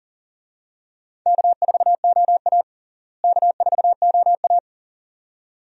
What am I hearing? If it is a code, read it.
Morse code, heard as a keyed tone, about 37 wpm, 700 Hz. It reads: K4OA K4OA
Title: USA Call Signs - 4 Characters - Repeated 2x - 40wpm